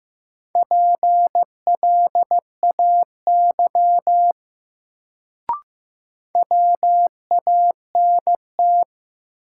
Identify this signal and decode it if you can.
Morse code, heard as a keyed tone, about 15 wpm, 700 Hz. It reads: PLAY WANT
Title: Sentences from Top 200 Words - New Word Drill - 15wpm